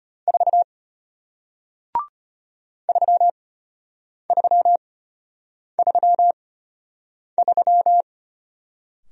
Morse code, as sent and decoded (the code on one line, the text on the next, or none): ....- / ...-- ...-- ...-- ...--
4 3333